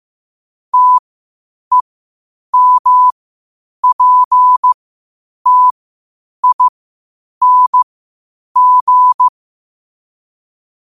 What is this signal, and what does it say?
Morse code, heard as a keyed tone, about 15 wpm, spaced as for 8 wpm, 1000 Hz. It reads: TEMPTING
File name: qotdfiles / 8wpm